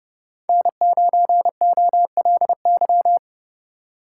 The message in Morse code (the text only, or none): N9OLY